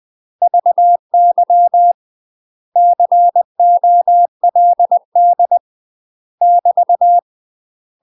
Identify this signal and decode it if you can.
Morse code, heard as a keyed tone, about 20 wpm, 700 Hz. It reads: VY COLD <BT>
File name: UK MASTER QSOs TRAINING / 20WPM / UK MASTER QSOs 0002